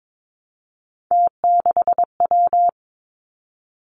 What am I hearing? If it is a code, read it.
Morse code, heard as a keyed tone, about 22 wpm, 700 Hz. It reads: T6W
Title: Three Challenging Letter-Number Combinations 22wpm